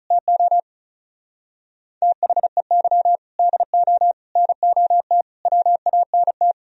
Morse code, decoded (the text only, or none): TO THEY DO NOT WANT